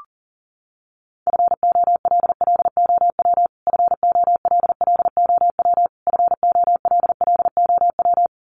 Morse code, heard as a keyed tone, about 40 wpm, 700 Hz. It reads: FOLLOW FOLLOW FOLLOW
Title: Top 200 Words Repeated 40wpm